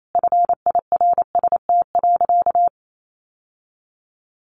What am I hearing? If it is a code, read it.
Morse code, heard as a keyed tone, about 28 wpm, 700 Hz. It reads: FIRST.
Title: Sentences from Top 1100 Words 28wpm